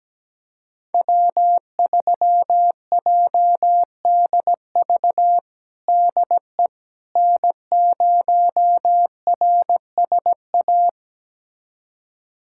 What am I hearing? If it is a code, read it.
Morse code, heard as a keyed tone, about 17 wpm, 700 Hz. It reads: W3JDV DE N0RSA
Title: USA Callsign DE Callsign - 5 Characters - 17wpm